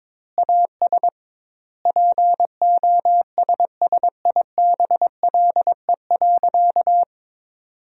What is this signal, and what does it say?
Morse code, heard as a keyed tone, about 22 wpm, 700 Hz. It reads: AS POSSIBLE.
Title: 2x Word-Spacing - Sentences from Top 500 Words - 22wpm